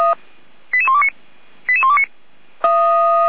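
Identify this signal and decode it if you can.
Five-tone selective call (ZVEI), heard as quick runs of five tones, tones of about 70 ms. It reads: 88129 88129
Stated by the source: ZVEI five-tone call 88419 received on an SDR.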